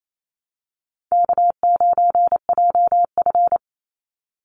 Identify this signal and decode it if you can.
Morse code, heard as a keyed tone, about 28 wpm, 700 Hz. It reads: K9JF
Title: Easy Call Signs - 28wpm